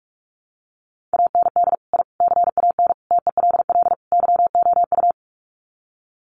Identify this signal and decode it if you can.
Morse code, heard as a keyed tone, about 44 wpm, 700 Hz. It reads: AND I CAN TELL YOU
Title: Sets of 5 Words Encore2 45wpm